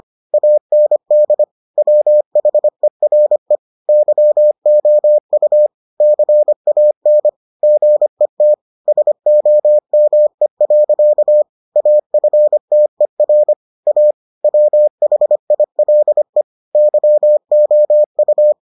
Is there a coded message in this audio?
Yes: Morse code from 0.3 to 18.5 s, heard as a keyed tone, about 25 wpm, 600 Hz. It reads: AND WHERE YOU CAN GET SOME. AFTER A WHILE YOU